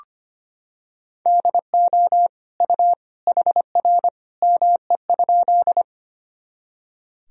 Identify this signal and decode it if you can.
Morse code, heard as a keyed tone, about 25 wpm, 700 Hz. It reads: DO U HR ME?
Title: Sentences from Top 300 Words and Abbreviations 25wpm